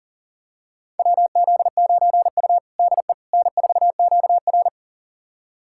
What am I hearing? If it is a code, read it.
Morse code, heard as a keyed tone, about 40 wpm, 700 Hz. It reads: WZ9U DE N4QR
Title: USA Callsign DE Callsign - 4 Characters - 40wpm